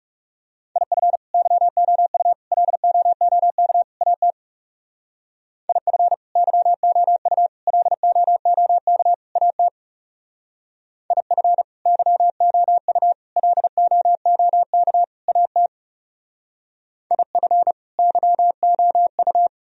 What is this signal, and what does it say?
Morse code, heard as a keyed tone, about 38 wpm, 700 Hz. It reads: IF YOU LOOK AT IF YOU LOOK AT IF YOU LOOK AT IF YOU
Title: Speed-Racing - Sets of 4 Words 30wpm